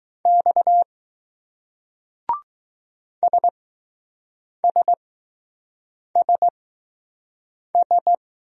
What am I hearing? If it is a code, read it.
Morse code, heard as a keyed tone, about 20 wpm, spaced as for 6 wpm, 700 Hz. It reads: X SSSS